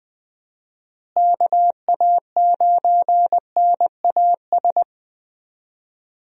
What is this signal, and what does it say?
Morse code, heard as a keyed tone, about 21 wpm, 700 Hz. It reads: KA9NAS
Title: USA Call Signs - 6 Characters - Repeated 2x - 20wpm